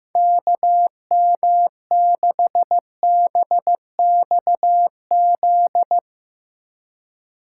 Morse code, heard as a keyed tone, about 15 wpm, 700 Hz. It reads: KM6BXZ
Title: USA Call Signs - 6 Characters - 2x Character Spacing - 15wpm